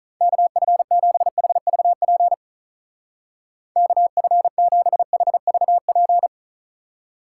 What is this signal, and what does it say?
Morse code, heard as a keyed tone, about 45 wpm, 700 Hz. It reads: KF7HVP KF7HVP